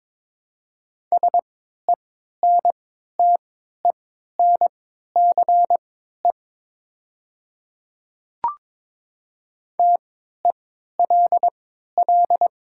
Morse code, heard as a keyed tone, about 22 wpm, spaced as for 13 wpm, 700 Hz. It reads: SENTENCE TELL
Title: 3x Character Spacing - Top 500 Words - 22wpm